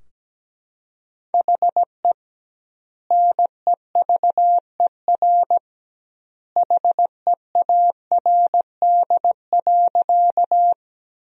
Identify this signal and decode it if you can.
Morse code, heard as a keyed tone, about 17 wpm, 700 Hz. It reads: HE NEVER HEARD.
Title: 2x Word-Spacing - Sentences from Top 400 Words - 17wpm